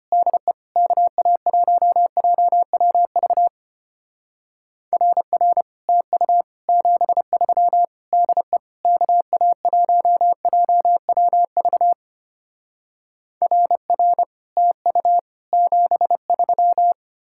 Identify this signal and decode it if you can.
Morse code, heard as a keyed tone, about 32 wpm, 700 Hz. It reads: DE KA1JWV RR TU 73 DE KA1JWV RR TU 73